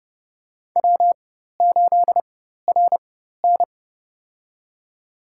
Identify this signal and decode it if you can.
Morse code, heard as a keyed tone, about 30 wpm, spaced as for 14 wpm, 700 Hz. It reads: W8RN
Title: USA Call Signs - 4 Characters - 4x Character Spacing - 30wpm